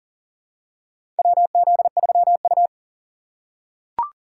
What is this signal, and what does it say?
Morse code, heard as a keyed tone, about 40 wpm, 700 Hz. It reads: WZ3U
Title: USA Call Signs - 4 Characters - 3x Character Spacing - 40wpm